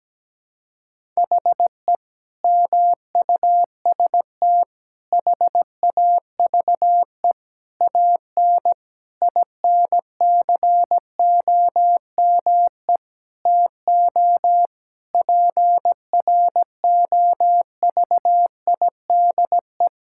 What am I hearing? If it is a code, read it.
Morse code, heard as a keyed tone, about 17 wpm, 700 Hz. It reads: HE MUST HAVE AN INCOME TO PROVIDE
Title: Sentences from Top 1000 Words - Review Part 5 - 17wpm